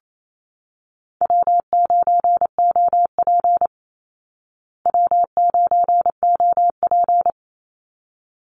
Morse code, heard as a keyed tone, about 28 wpm, 690 Hz. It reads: W9OP W9OP
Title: Call Signs Repeated 3x - 28wpm